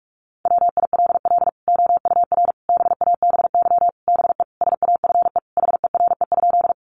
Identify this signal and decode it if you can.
Morse code, heard as a keyed tone, about 44 wpm, 700 Hz. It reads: WILL OUR BABY BE SAFE HERE?
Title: Sentences from Top 1300 Words 45wpm